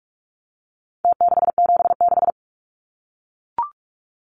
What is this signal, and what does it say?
Morse code, heard as a keyed tone, about 44 wpm, 700 Hz. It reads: T676
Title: Numbers - Contest Style - 4-Digit 45wpm